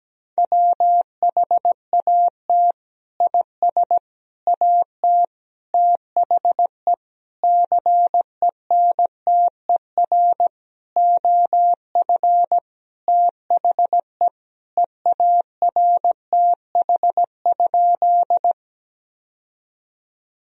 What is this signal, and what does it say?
Morse code, heard as a keyed tone, about 17 wpm, 700 Hz. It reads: WHAT IS AT THE CENTER OF THE EARTH?
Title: Sentences from Top 1000 Words - Review Part 2 - 17wpm